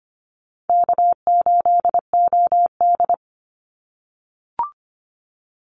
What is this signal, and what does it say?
Morse code, heard as a keyed tone, about 25 wpm, 700 Hz. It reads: K8OD
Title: Easy Call Signs - 25wpm